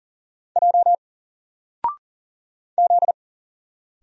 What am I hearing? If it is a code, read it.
Morse code, heard as a keyed tone, about 40 wpm, 700 Hz. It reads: J Z
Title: Single Letter-Number Alt 40wpm